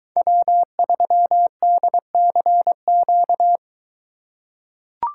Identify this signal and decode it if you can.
Morse code, heard as a keyed tone, about 22 wpm, 700 Hz. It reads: W3DCQ